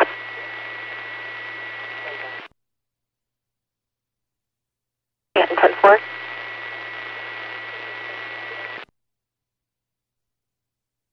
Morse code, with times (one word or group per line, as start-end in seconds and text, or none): none